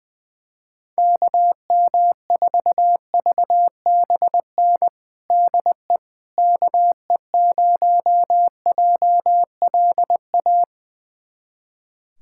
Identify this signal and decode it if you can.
Morse code, heard as a keyed tone, about 21 wpm, 700 Hz. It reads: KM4VBN DE KE0JLA
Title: USA Callsign DE Callsign - 6 Characters - 20wpm